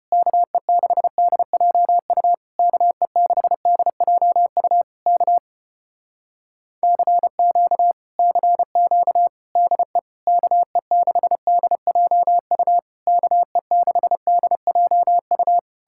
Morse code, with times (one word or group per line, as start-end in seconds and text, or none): 0.1-2.3 KE6DJU
2.6-4.8 KE6DJU
5.1-5.4 K
6.8-7.9 CQ
8.2-9.3 CQ
9.6-10.0 DE
10.3-12.8 KE6DJU
13.1-15.6 KE6DJU